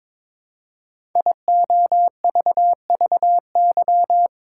Morse code, heard as a keyed tone, about 22 wpm, 700 Hz. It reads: IOVVY
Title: Code Groups - 5 Letter:Numbers - 22wpm